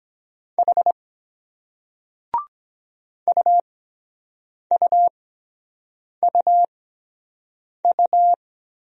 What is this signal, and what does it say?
Morse code, heard as a keyed tone, about 23 wpm, spaced as for 7 wpm, 700 Hz. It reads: H UUUU